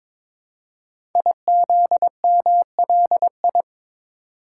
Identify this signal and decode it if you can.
Morse code, heard as a keyed tone, about 22 wpm, 700 Hz. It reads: IZMLI
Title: Code Groups - 5 Letters - 22wpm